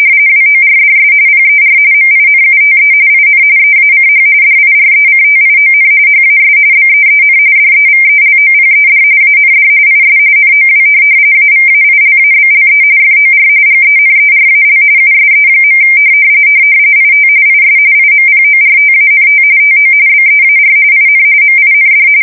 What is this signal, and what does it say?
RTTY, about 45.45 baud. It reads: WORDS PER MINUTE IN EACH DIRECTION HAS BEEN DEVELOPED AND IS NOW AVAILABLE FOR GENERAL USE. IT IS IMPORTANT TO BEAR IN MIND
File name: fsk170 / printing-telegraph